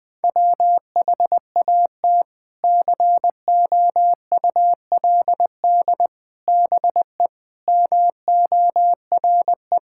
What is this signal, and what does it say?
Morse code, heard as a keyed tone, about 20 wpm, 700 Hz. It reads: WHAT COULD BE MORE
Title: Sentences from Top 1000 Words - Review Part 6 - 20wpm